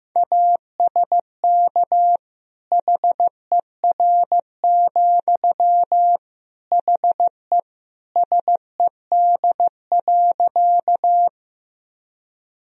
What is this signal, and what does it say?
Morse code, heard as a keyed tone, about 15 wpm, 700 Hz. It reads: ASK HER, HE SED.
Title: Sentences from Top 200 Words and Abbreviations 15wpm